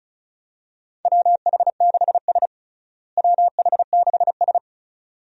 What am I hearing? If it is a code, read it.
Morse code, heard as a keyed tone, about 36 wpm, 700 Hz. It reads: WH6S WH6S
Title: USA Call Signs - 4 Characters - Repeated 2x - 35wpm